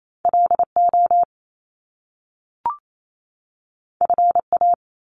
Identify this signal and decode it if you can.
Morse code, heard as a keyed tone, about 28 wpm, 700 Hz. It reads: L O F A
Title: Top Two Letter Combinations in Words 28wpm